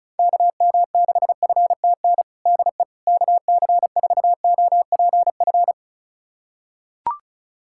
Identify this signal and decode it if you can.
Morse code, heard as a keyed tone, about 36 wpm, 700 Hz. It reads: KM6FTN DE KC4OPF
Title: USA Callsign DE Callsign 35wpm